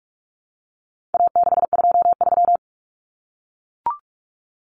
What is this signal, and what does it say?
Morse code, heard as a keyed tone, about 44 wpm, 700 Hz. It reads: A623
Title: Common Cut Numbers - 4-Digit 45wpm